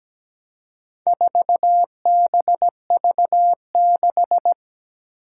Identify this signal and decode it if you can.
Morse code, heard as a keyed tone, about 17 wpm, 700 Hz. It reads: 4BV6
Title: Four Challenging Letter-Number Combinations 17wpm